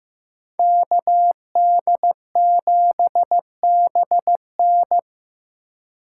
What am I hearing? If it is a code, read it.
Morse code, heard as a keyed tone, about 18 wpm, 700 Hz. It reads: KD7BN
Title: Speed Racing - USA 2x2 Call Signs 15wpm